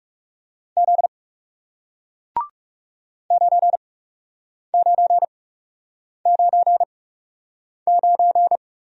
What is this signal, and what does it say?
Morse code, heard as a keyed tone, about 40 wpm, spaced as for 8 wpm, 700 Hz. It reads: Z 9999